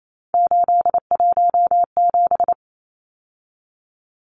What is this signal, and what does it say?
Morse code, heard as a keyed tone, about 28 wpm, 700 Hz. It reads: 817
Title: Numbers - 3-Digit 28wpm